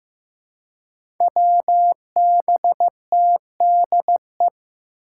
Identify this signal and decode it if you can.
Morse code, heard as a keyed tone, about 15 wpm, 700 Hz. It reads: WBTDE
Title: Five Challenging Letter-Number Combinations 15wpm